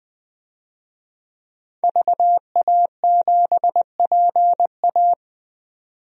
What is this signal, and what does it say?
Morse code, heard as a keyed tone, about 20 wpm, 700 Hz. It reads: VA7PA
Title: DXCC-100 Call Signs - 5 Characters - 20wpm